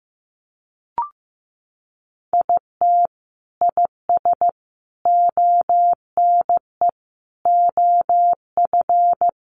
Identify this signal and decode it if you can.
Morse code, heard as a keyed tone, about 15 wpm, 700 Hz. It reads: IT IS ONE OF
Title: Sets of 5 Words 15wpm